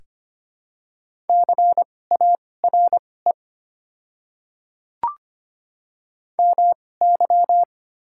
Morse code, heard as a keyed tone, about 25 wpm, spaced as for 18 wpm, 700 Hz. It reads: CARE MY
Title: 2x Character Spacing - Top 500 Words - 25wpm